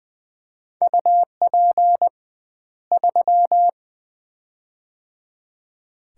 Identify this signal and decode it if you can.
Morse code, heard as a keyed tone, about 20 wpm, 700 Hz. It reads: UP 3
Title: Common QSO Phrases - 2x Word-Spacing - 20wpm